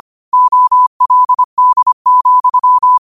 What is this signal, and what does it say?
Morse code, heard as a keyed tone, about 25 wpm, 1000 Hz. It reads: OLD,
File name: qotdfiles / 25wpm